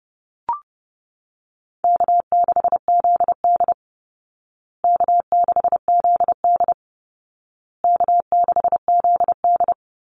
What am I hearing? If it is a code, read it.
Morse code, heard as a keyed tone, about 30 wpm, 700 Hz. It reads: K6ZD K6ZD K6ZD
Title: Easy Call Signs Repeated 3x - 30wpm